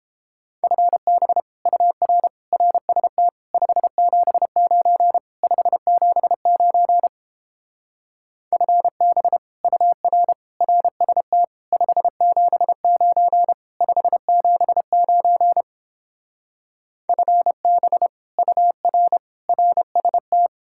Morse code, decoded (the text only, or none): FB UR RST 579 579 FB UR RST 579 579 FB UR RST